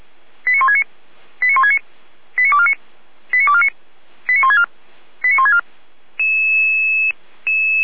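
Five-tone selective call (ZVEI), heard as quick runs of five tones, tones of about 80 ms. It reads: 89169 89169 89249 89249 89164 89164
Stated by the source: ZVEI five-tone selective call 89560, received via SDR.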